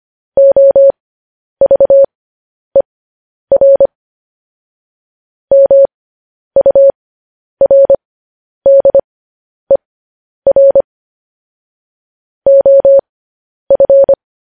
Morse code, heard as a keyed tone, about 25 wpm, spaced as for 10 wpm, 550 Hz. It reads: OVER MURDER OF
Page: news-headlines-in-morse-code-at-10-wpm